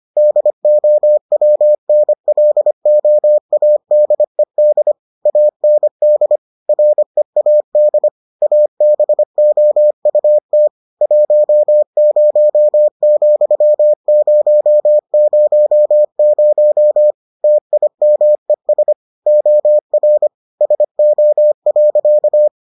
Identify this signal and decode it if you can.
Morse code, heard as a keyed tone, about 25 wpm, 600 Hz. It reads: DOWNLOADED AND READ ABOUT 10,000 TIMES OR SO.